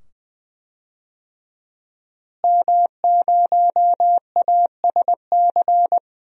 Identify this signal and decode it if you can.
Morse code, heard as a keyed tone, about 20 wpm, 700 Hz. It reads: M0ASC
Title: DXCC-100 Call Signs - 5 Characters - Repeated 2x - 20wpm